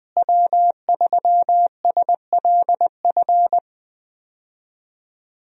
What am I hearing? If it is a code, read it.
Morse code, heard as a keyed tone, about 20 wpm, 700 Hz. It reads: W3SLF
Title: USA Call Signs - 5 Characters - Repeated 3x - 20wpm